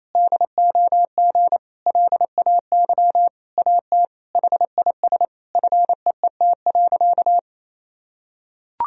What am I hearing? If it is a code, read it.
Morse code, heard as a keyed tone, about 27 wpm, 700 Hz. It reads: DOG LAY AT HIS FEET.
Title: Sentences from Top 1000 Words - Review Part 2 - 28wpm